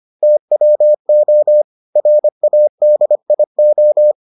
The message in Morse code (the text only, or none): TWO RADIO